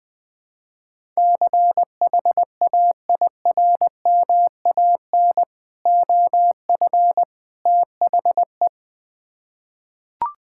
Morse code, heard as a keyed tone, about 20 wpm, 700 Hz. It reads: CHAIRMAN OF THE
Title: Build A Phrase - 3-Word Phrases - 20wpm